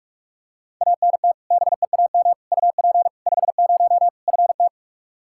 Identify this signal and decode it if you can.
Morse code, heard as a keyed tone, about 44 wpm, 700 Hz. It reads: ANT BEAM UP 50 FT
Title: Practice QSO - Exchange 2 - Info - 45wpm